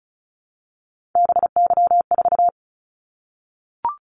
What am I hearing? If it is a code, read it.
Morse code, heard as a keyed tone, about 36 wpm, 700 Hz. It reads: BY4
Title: Callsign Prefixes for Top 100 DXCC - 35wpm